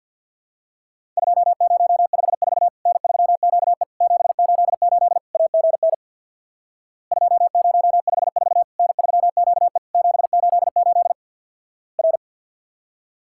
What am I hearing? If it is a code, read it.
Morse code, heard as a keyed tone, about 50 wpm, 700 Hz. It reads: 2054 N3QE 788 AGN 2054 N3QE 788 R
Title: Worked All Europe DX Contest - QTC Exchange with Repeats - 50wpm